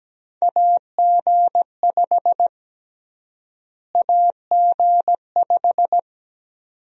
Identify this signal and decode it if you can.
Morse code, heard as a keyed tone, about 17 wpm, 700 Hz. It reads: AG5 AG5
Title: USA Call Sign Prefixes - Repeated 2x - 17wpm